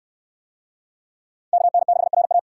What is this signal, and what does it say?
Morse code, heard as a keyed tone, about 69 wpm, 700 Hz. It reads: KN6RA